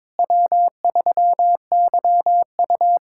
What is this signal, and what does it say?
Morse code, heard as a keyed tone, about 22 wpm, 700 Hz. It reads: W3YU
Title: USA Callsign DE Callsign - 4 Characters - 22wpm